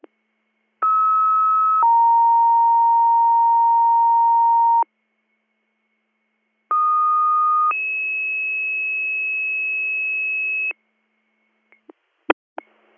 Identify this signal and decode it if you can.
Two-tone page, heard as a one-second tone, then a three-second tone, about 1290 then 930 Hz, about 1230 then 2600 Hz. It is heard from 0.8 to 10.7 s.